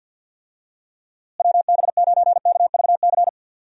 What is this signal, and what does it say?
Morse code, heard as a keyed tone, about 50 wpm, 700 Hz. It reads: WB9KVC
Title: USA Call Signs - 6 Characters - Repeated 3x - 50wpm